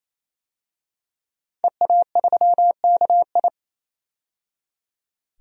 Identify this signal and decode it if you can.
Morse code, heard as a keyed tone, about 28 wpm, 700 Hz. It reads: EA3KI
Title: DXCC-100 Call Signs - 5 Characters - 28wpm